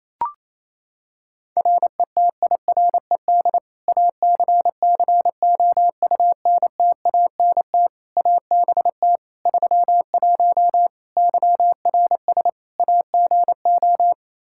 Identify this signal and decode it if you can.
Morse code, heard as a keyed tone, about 28 wpm, 700 Hz. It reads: RETIRED ACCOUNTANT ABT 31 YRS AGO
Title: K1USN SST QSO Practice 28wpm